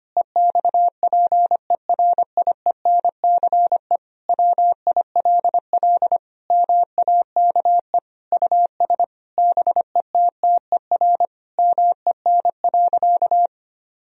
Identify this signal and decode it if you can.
Morse code, heard as a keyed tone, about 25 wpm, 700 Hz. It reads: EXPERIENCE WILL MAKE US BETTER MEN.
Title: Sentences from Top 1000 Words - Review Part 3 - 25wpm